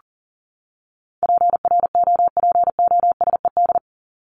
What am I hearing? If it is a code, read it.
Morse code, heard as a keyed tone, about 40 wpm, 700 Hz. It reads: PROPOSED
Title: ED Suffix - Top 100 Words with Base Words 40wpm